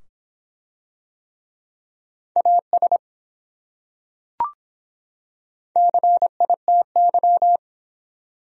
Morse code, heard as a keyed tone, about 24 wpm, 700 Hz. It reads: AS CITY